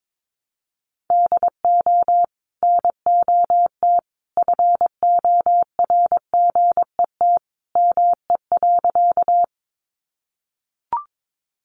Morse code, heard as a keyed tone, about 22 wpm, 700 Hz. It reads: DO NOT FORGET ME.
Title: Sentences from Top 1200 Words 22wpm